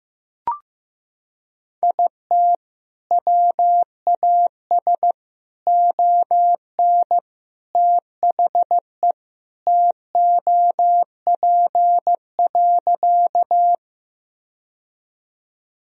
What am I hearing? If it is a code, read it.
Morse code, heard as a keyed tone, about 15 wpm, 700 Hz. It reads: IT WAS ON THE TOP.
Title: Sentences from Top 1000 Words - Review Part 2 - 15wpm